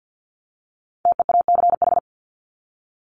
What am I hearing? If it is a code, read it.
Morse code, heard as a keyed tone, about 48 wpm, 700 Hz. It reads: TEACH
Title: ING Suffix - Top 100 Words with Base Words 50wpm